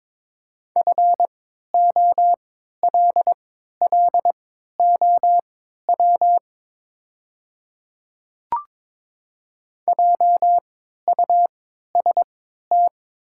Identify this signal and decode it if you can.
Morse code, heard as a keyed tone, about 22 wpm, spaced as for 13 wpm, 700 Hz. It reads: FOLLOW JUST
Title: 3x Character Spacing - Top 200 Words - 22wpm